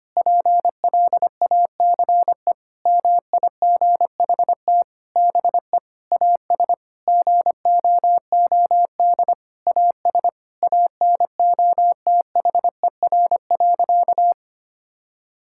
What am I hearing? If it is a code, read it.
Morse code, heard as a keyed tone, about 25 wpm, 700 Hz. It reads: PLACE MIGHT BE AS GOOD AS ANOTHER.
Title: Sentences from Top 1600 Words 25wpm